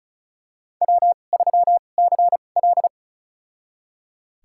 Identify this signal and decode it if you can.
Morse code, heard as a keyed tone, about 36 wpm, spaced as for 25 wpm, 700 Hz. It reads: W3CL
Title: USA Call Signs - 4 Characters - 2x Character Spacing - 35wpm